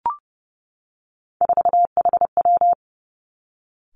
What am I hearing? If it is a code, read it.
Morse code, heard as a keyed tone, about 30 wpm, 700 Hz. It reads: 4HW
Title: Three Challenging Letter-Number Combinations 30wpm